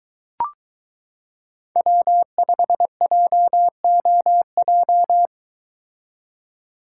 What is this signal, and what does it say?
Morse code, heard as a keyed tone, about 22 wpm, 700 Hz. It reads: W5JOJ